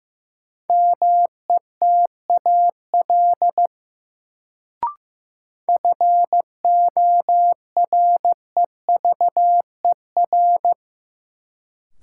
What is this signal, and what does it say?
Morse code, heard as a keyed tone, about 15 wpm, 700 Hz. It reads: METAL FOREVER
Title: Sentences from Top 1600 Words - New Word Drill Alt - 15wpm